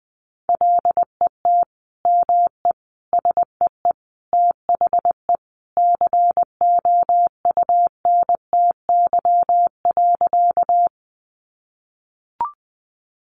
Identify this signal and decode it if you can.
Morse code, heard as a keyed tone, about 20 wpm, 700 Hz. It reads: LET ME SEE THE COUNTY.
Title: Sentences from Top 1200 Words 20wpm